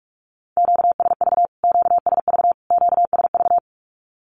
Morse code, heard as a keyed tone, about 45 wpm, 700 Hz. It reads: QSV QSV QSV
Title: Q-Codes Repeated 3x 45wpm